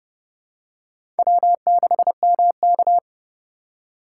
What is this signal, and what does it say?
Morse code, heard as a keyed tone, about 29 wpm, 700 Hz. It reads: W6MK
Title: USA Call Signs - 4 Characters - 2x Character Spacing - 30wpm